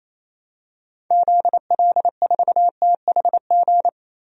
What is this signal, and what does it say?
Morse code, heard as a keyed tone, about 28 wpm, 700 Hz. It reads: ZL4THG
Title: DXCC-100 Call Signs - 6 Characters - 28wpm